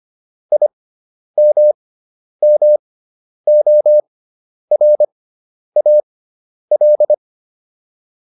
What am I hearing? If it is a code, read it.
Morse code, heard as a keyed tone, about 25 wpm, spaced as for 10 wpm, 600 Hz. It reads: IMMORAL